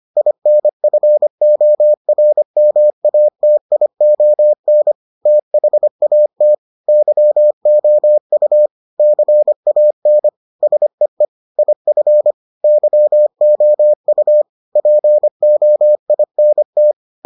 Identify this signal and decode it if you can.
Morse code, heard as a keyed tone, about 25 wpm, 600 Hz. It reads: INFORMATION THAT YOU CAN SEE IF YOU POINT